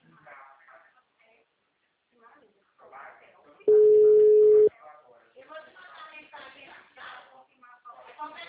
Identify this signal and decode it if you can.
Ringback tone (Europe, 425 Hz), with speech.